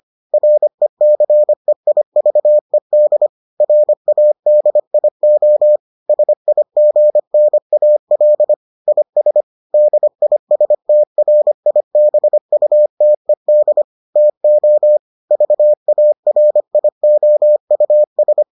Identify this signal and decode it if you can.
Morse code, heard as a keyed tone, about 25 wpm, 600 Hz. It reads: RECEIVED RADIO SIGNAL IS DISTRIBUTED TO VARIOUS